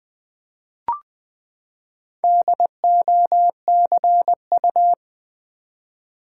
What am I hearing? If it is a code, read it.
Morse code, heard as a keyed tone, about 20 wpm, 700 Hz. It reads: DOCU
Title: Build A Word - 4 Letters At A Time - 11-Letter Words - 20wpm